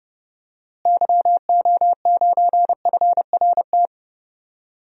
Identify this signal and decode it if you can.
Morse code, heard as a keyed tone, about 29 wpm, 700 Hz. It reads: YO9FRT
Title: DXCC-100 Call Signs - 6 Characters - Repeated 3x - 30wpm